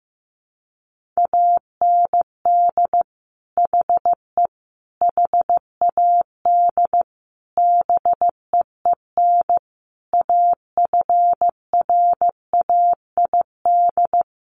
Morse code, heard as a keyed tone, about 15 wpm, 700 Hz. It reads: AND HE HAD BEEN AFRAID
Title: Sentences from Top 1500 Words 15wpm